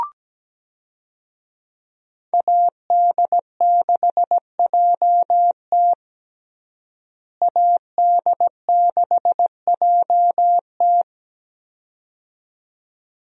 Morse code, heard as a keyed tone, about 17 wpm, 700 Hz. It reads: AD6JT AD6JT
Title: USA Call Signs - 5 Characters - Repeated 2x - 17wpm